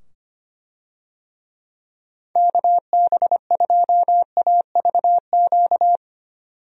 Morse code, heard as a keyed tone, about 25 wpm, 700 Hz. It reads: KB2AVQ
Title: USA Call Signs - 6 Characters - 25wpm